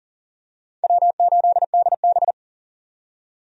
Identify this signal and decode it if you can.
Morse code, heard as a keyed tone, about 40 wpm, 700 Hz. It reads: W8DB
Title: USA Call Signs - 4 Characters - 40wpm